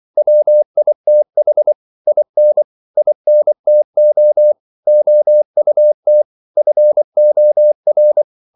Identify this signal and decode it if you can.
Morse code, heard as a keyed tone, about 24 wpm, 600 Hz. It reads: WITH IN INTO OUT FOR
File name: CW / Practice / 500MostCommonEnglishWords24